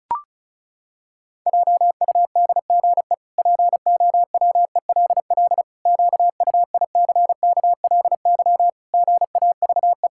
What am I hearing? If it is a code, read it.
Morse code, heard as a keyed tone, about 36 wpm, 700 Hz. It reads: JUDGE POWELL QUICKLY GAVE
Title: Pangram Challenge v2 - 35wpm